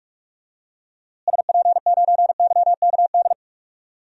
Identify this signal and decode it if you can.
Morse code, heard as a keyed tone, about 47 wpm, 700 Hz. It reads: SP9YKD
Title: DXCC-100 Call Signs - 6 Characters - Repeated 2x - 45wpm